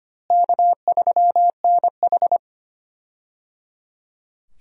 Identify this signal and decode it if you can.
Morse code, heard as a keyed tone, about 25 wpm, 700 Hz. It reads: K3NH
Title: USA Call Signs - 4 Characters - Repeated 3x - 25wpm